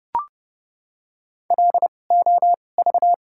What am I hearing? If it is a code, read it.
Morse code, heard as a keyed tone, about 30 wpm, spaced as for 22 wpm, 690 Hz. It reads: LOV
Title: 2x Character Spacing - Top 500 Words - 30wpm